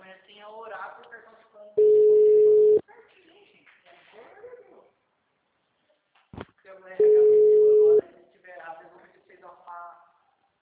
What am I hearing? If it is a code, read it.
Ringback tone (Europe, 425 Hz), with speech.